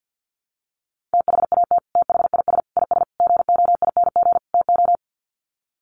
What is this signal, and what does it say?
Morse code, heard as a keyed tone, about 50 wpm, 700 Hz. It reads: THAT THIS IS GOING TO
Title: Sets of 5 Words Encore2 50wpm